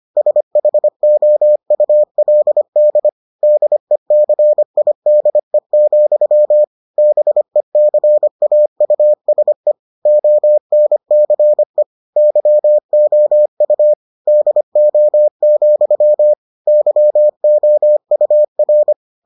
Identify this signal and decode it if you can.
Morse code, heard as a keyed tone, about 25 wpm, 600 Hz. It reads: SHOULD DECIDE, BECAUSE ONCE YOU DO, YOUR